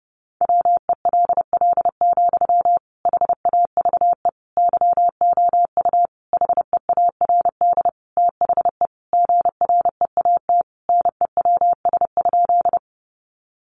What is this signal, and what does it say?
Morse code, heard as a keyed tone, about 30 wpm, 700 Hz. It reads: WELL, HAVE YOU HEARD THE GREAT NEWS?
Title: Sentences from Top 800 Words 30wpm